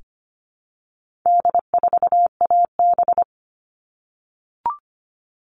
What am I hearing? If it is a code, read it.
Morse code, heard as a keyed tone, about 25 wpm, 700 Hz. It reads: D4AB
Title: All Cut Numbers - 4-Digit 25wpm